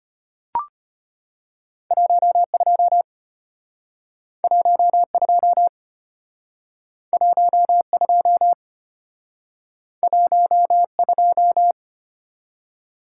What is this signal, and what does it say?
Morse code, heard as a keyed tone, about 31 wpm, 700 Hz. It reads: 12 12 12 12
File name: practice / 25wpm / Speed-Racing - Two Letter-Number Combinations - 25wpm